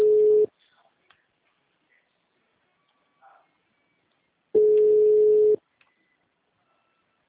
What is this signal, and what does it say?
Ringback tone (Europe, 425 Hz).